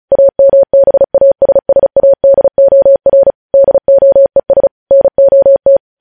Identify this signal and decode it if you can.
Morse code, heard as a keyed tone, about 36 wpm, 550 Hz. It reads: AMBASSADOR DOES NOT
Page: News Headlines in Morse Code at 35 WPM